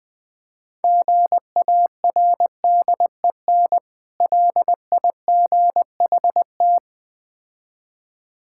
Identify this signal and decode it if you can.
Morse code, heard as a keyed tone, about 20 wpm, 700 Hz. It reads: GARDEN LIGHT
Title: Easy Words - Speed-Racing with Hints - 20wpm